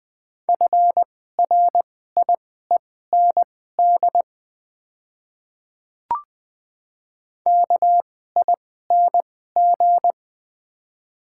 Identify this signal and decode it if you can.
Morse code, heard as a keyed tone, about 20 wpm, spaced as for 14 wpm, 700 Hz. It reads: FRIEND KING
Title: 2x Character Spacing - Top 500 Words - 20wpm